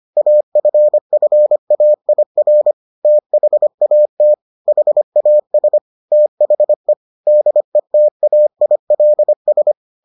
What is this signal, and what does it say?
Morse code, heard as a keyed tone, about 25 wpm, 600 Hz. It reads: AFFAIR THAT HAS THE DETAILS